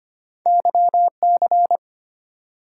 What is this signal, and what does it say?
Morse code, heard as a keyed tone, about 25 wpm, 700 Hz. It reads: YC
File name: Two Letter-Number Combinations - 25wpm